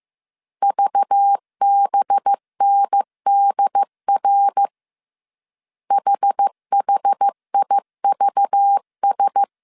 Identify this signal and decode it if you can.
Morse code, heard as a keyed tone, about 15 wpm, 800 Hz. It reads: VBNDR HHIVS
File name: CW / DET / 1 6 1500C S A 14wpm 2921